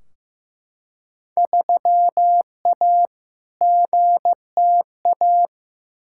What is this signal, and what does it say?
Morse code, heard as a keyed tone, about 15 wpm, 700 Hz. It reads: 3A GTA
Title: Field Day Exchanges - 15wpm